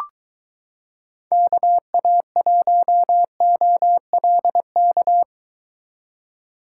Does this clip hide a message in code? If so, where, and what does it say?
Yes: Morse code from 1.3 to 5.2 s, heard as a keyed tone, about 23 wpm, 700 Hz. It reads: KA1OLK